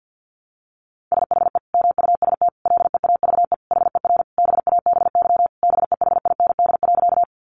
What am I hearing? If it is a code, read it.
Morse code, heard as a keyed tone, about 50 wpm, 700 Hz. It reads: SHE MUST LEAVE HER BABY BEHIND.
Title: Sentences from Top 1300 Words 50wpm